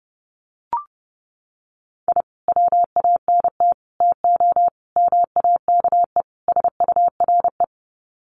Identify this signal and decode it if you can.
Morse code, heard as a keyed tone, about 30 wpm, 700 Hz. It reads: I WANT TO MAKE SURE
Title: Sets of 5 Words Encore2 30wpm